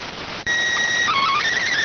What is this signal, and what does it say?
SSTV (slow-scan television), PD 290.